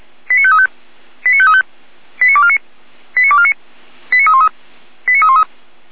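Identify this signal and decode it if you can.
Five-tone selective call (ZVEI), heard as quick runs of five tones, tones of about 70 ms. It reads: 89535 89535 89249 89249 89313 89313
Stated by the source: ZVEI five-tone selective call received with an SDR.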